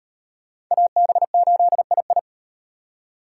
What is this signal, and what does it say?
Morse code, heard as a keyed tone, about 36 wpm, 700 Hz. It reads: AB8II